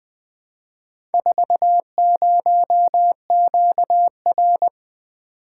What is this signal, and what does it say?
Morse code, heard as a keyed tone, about 20 wpm, 700 Hz. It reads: 40QR
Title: Four Letter-Number Combinations - 20wpm